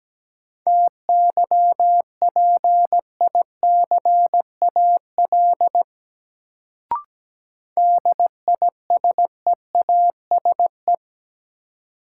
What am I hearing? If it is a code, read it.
Morse code, heard as a keyed tone, about 17 wpm, 700 Hz. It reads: TYPICAL DISEASE
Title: Sentences from Top 1100 Words - New Word Drill Alt - 17wpm